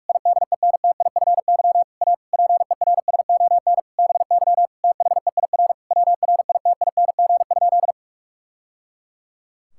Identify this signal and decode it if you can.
Morse code, heard as a keyed tone, about 46 wpm, 700 Hz. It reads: IDENTIFY A PERSON BY THEIR WRITING?
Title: Sentences from Top 1000 Words - Review Part 6 - 45wpm